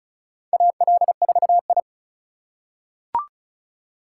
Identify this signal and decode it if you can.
Morse code, heard as a keyed tone, about 36 wpm, 700 Hz. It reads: AL4I
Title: USA Call Signs - 4 Characters - 2x Character Spacing - 35wpm